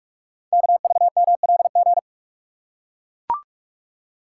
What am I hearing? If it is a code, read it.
Morse code, heard as a keyed tone, about 44 wpm, 700 Hz. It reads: KVMLG
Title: Code Groups - 5 Letters - 45wpm